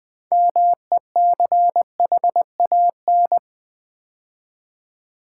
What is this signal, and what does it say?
Morse code, heard as a keyed tone, about 20 wpm, 700 Hz. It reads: MECHAN
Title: Build A Word - 3 Letters At A Time - 8-Letter Words - 20wpm